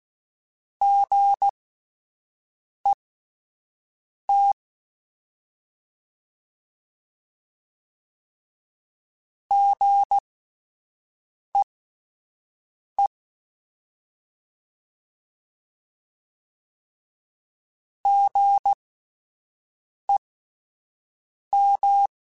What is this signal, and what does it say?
Morse code, heard as a keyed tone, about 16 wpm, spaced as for 5 wpm, 800 Hz. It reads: GET GEE GEM
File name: Spaced Alphas / Spaced Alphas mp3s / Gs sp @10